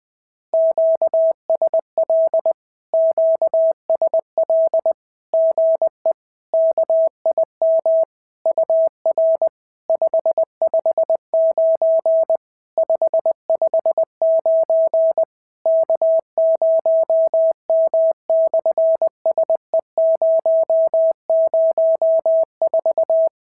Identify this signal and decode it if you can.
Morse code, heard as a keyed tone, about 20 wpm, 650 Hz. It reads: QSL QSL GE KIM UR 559 559 K0M/SE004
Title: SOTA QSO Practice 20wpm